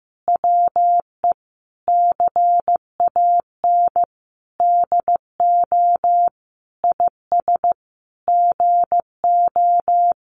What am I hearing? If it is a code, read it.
Morse code, heard as a keyed tone, about 15 wpm, 700 Hz. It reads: WE CAN DO IS GO
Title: Sentences from Top 200 Words 15wpm